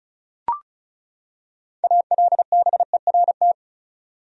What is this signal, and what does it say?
Morse code, heard as a keyed tone, about 36 wpm, 700 Hz. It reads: ALBERT
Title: USA Common Names - 35wpm